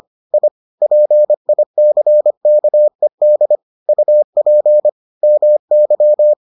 Morse code, heard as a keyed tone, about 25 wpm, 600 Hz. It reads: I PICKED UP MY